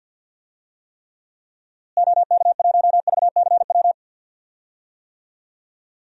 Morse code, heard as a keyed tone, about 52 wpm, 700 Hz. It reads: OK1VCW
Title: DXCC-100 Call Signs - 6 Characters - Repeated 3x - 50wpm